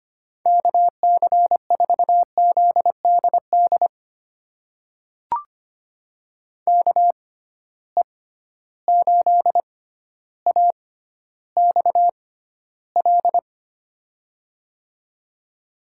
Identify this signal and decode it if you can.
Morse code, heard as a keyed tone, about 25 wpm, spaced as for 13 wpm, 700 Hz. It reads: KC4ZDD KE8AXL
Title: USA Call Signs - 6 Characters - 6x Character Spacing - 25wpm